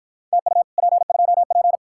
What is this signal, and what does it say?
Morse code, heard as a keyed tone, about 49 wpm, 700 Hz. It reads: TU P2P